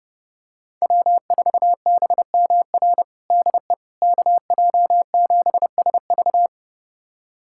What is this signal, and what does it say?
Morse code, heard as a keyed tone, about 30 wpm, 700 Hz. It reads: W4BMR DE KJ7SV
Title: USA Callsign DE Callsign - 5 Characters - 30wpm